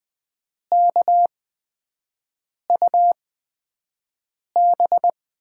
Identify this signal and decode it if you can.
Morse code, heard as a keyed tone, about 20 wpm, spaced as for 5 wpm, 700 Hz. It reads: KUB
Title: USA Call Signs - 6 Characters - 8x Character Spacing - 20wpm